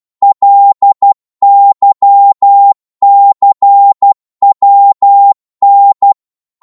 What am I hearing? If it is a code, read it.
Morse code, heard as a keyed tone, about 12 wpm, 800 Hz. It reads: LYCWN